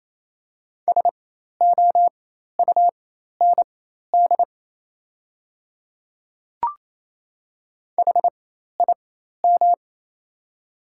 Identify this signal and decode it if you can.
Morse code, heard as a keyed tone, about 27 wpm, spaced as for 13 wpm, 700 Hz. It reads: SOUND HIM
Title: 4x Character Spacing - Top 100 Words - 28wpm